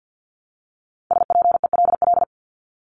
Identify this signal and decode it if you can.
Morse code, heard as a keyed tone, about 50 wpm, 700 Hz. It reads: SPELL
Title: Top 500 Words Repeated 50wpm